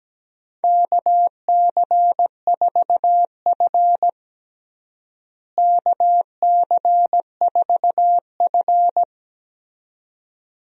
Morse code, heard as a keyed tone, about 17 wpm, 700 Hz. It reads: KC4F KC4F
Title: USA Call Signs - 4 Characters - Repeated 2x - 17wpm